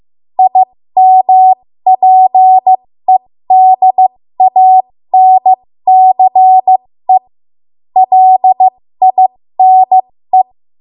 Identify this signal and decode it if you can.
Morse code, heard as a keyed tone, about 15 wpm, spaced as for 13 wpm, 750 Hz. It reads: IMPEDANCE LINE
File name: Morse / 13WPM